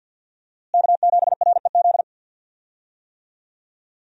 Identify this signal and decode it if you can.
Morse code, heard as a keyed tone, about 50 wpm, 700 Hz. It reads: K7REZ
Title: USA Call Signs - 5 Characters - Repeated 2x - 50wpm